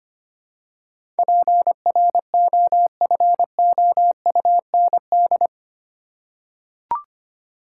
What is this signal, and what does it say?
Morse code, heard as a keyed tone, about 25 wpm, 700 Hz. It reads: PROFOUND
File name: 8 Letter Words - First 3 Letters Given - 25wpm